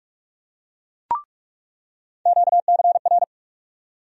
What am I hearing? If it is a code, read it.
Morse code, heard as a keyed tone, about 46 wpm, 700 Hz. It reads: QCR
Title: USA Call Sign Postfixes - 3 Letters 45wpm